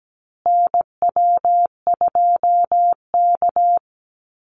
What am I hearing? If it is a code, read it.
Morse code, heard as a keyed tone, about 17 wpm, 700 Hz. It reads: NW2K
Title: Call Signs Repeated 2x - 17wpm